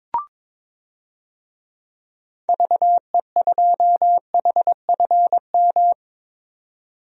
Morse code, heard as a keyed tone, about 21 wpm, 700 Hz. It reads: VE2HFM
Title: DXCC-100 Call Signs - 6 Characters - 22wpm